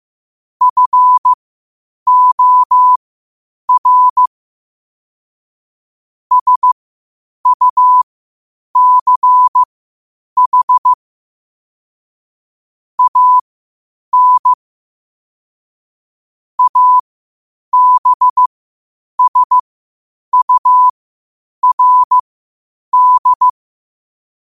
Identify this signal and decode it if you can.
Morse code, heard as a keyed tone, about 15 wpm, spaced as for 8 wpm, 1000 Hz. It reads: FOR SUCH AN ABSURD
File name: qotdfiles / 8wpm